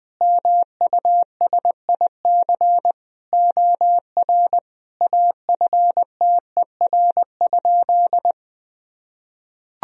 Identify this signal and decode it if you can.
Morse code, heard as a keyed tone, about 21 wpm, 700 Hz. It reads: MUSIC OR AFTER?
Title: Sentences from Top 1000 Words - Review Part 1 - 20wpm